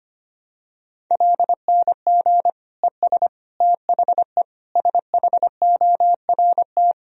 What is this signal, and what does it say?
Morse code, heard as a keyed tone, about 25 wpm, 700 Hz. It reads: LNG ES THE SHORT
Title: Sentences from Top 400 Words and Abbreviations 25wpm